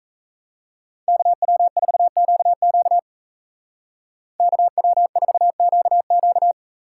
Morse code, heard as a keyed tone, about 34 wpm, 700 Hz. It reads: KW4QQ KW4QQ
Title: Speed Racing - USA Call Signs 28wpm